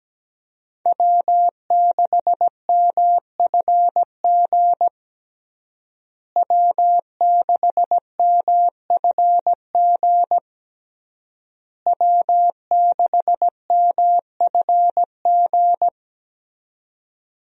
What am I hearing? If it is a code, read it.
Morse code, heard as a keyed tone, about 17 wpm, 700 Hz. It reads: W6MFG W6MFG W6MFG
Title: USA Call Signs - 5 Characters - Repeated 3x - 17wpm